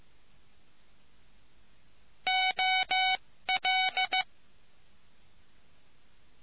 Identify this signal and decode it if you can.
Morse code, heard as a keyed tone, about 15 wpm, spaced as for 13 wpm, 750 Hz. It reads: OL